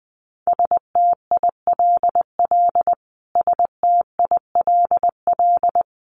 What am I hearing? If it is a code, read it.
Morse code, heard as a keyed tone, about 20 wpm, 700 Hz. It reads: STILL STILL
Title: Top 500 Words Repeated 20wpm